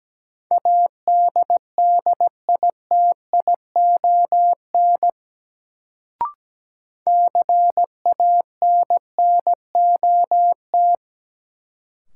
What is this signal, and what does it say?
Morse code, heard as a keyed tone, about 17 wpm, 700 Hz. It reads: ADDITION CANNOT
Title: Sentences from Top 1000 Words - Review Part 6 - New Word Drill - 17wpm